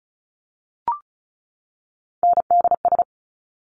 Morse code, heard as a keyed tone, about 36 wpm, 700 Hz. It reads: NDS
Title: Top Three Letter Combinations in Words 35wpm